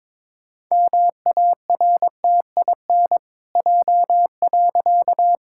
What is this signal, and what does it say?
Morse code, heard as a keyed tone, about 22 wpm, 700 Hz. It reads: MARTIN J.
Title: Pangram Challenge v2 - 22wpm